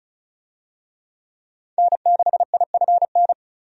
Morse code, heard as a keyed tone, about 34 wpm, 700 Hz. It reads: N6IFN
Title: USA Call Signs - 5 Characters - 35wpm